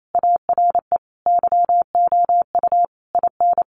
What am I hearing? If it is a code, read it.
Morse code, heard as a keyed tone, about 28 wpm, 700 Hz. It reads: ARE YOU IN
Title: Sentences from Top 1200 Words 28wpm